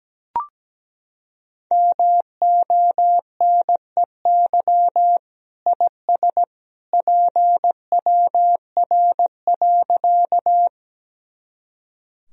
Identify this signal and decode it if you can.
Morse code, heard as a keyed tone, about 17 wpm, 700 Hz. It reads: MONEY IS PWR.
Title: Sentences from Top 500 Words and Abbreviations 17wpm